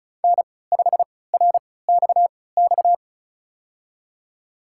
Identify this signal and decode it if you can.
Morse code, heard as a keyed tone, about 35 wpm, spaced as for 20 wpm, 700 Hz. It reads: N5RXX
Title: USA Call Signs - 3x Character Spacing - 35wpm